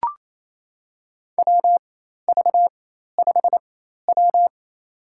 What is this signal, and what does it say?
Morse code, heard as a keyed tone, about 28 wpm, spaced as for 13 wpm, 700 Hz. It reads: WV5W
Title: USA Call Signs - 4 Characters - 4x Character Spacing - 28wpm